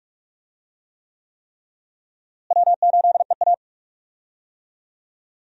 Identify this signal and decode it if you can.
Morse code, heard as a keyed tone, about 45 wpm, 700 Hz. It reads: W8EA
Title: USA Call Signs - 4 Characters - Repeated 3x - 45wpm